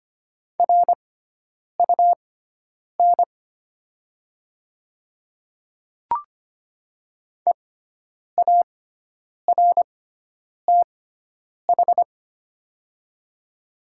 Morse code, heard as a keyed tone, about 25 wpm, spaced as for 9 wpm, 700 Hz. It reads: RUN EARTH
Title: 6x Character Spacing - Top 300 Words - 25wpm